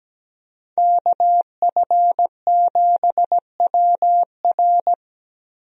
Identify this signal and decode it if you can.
Morse code, heard as a keyed tone, about 17 wpm, 700 Hz. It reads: KF7WR
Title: USA Call Signs - 5 Characters - Repeated 3x - 17wpm